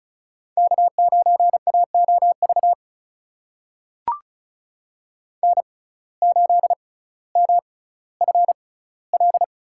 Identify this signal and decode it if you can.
Morse code, heard as a keyed tone, about 36 wpm, spaced as for 18 wpm, 700 Hz. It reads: K9AOV N8MFL
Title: USA Call Signs - 5 Characters - 6x Character Spacing - 35wpm